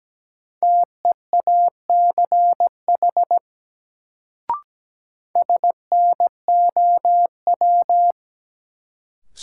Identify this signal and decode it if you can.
Morse code, heard as a keyed tone, about 17 wpm, 700 Hz. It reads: TEACH SNOW
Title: Sentences from Top 500 Words - New Word Drill - 17wpm